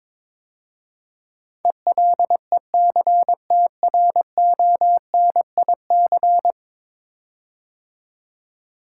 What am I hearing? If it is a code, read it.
Morse code, heard as a keyed tone, about 22 wpm, 700 Hz. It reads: ELECTRONIC
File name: practice / 22wpm / 10 Letter Words - First 3 Letters Given - 22wpm